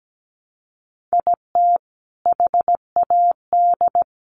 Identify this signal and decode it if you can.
Morse code, heard as a keyed tone, about 17 wpm, 700 Hz. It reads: IT HAD
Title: Sets of 5 Words Encore2 17wpm